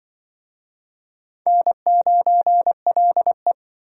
Morse code, heard as a keyed tone, about 25 wpm, 700 Hz. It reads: N9LE
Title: Speed Racing - USA 1x2 Call Signs 20wpm